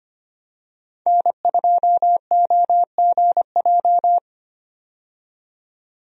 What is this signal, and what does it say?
Morse code, heard as a keyed tone, about 25 wpm, 690 Hz. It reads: N2OGJ
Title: USA Call Signs - 5 Characters - 25wpm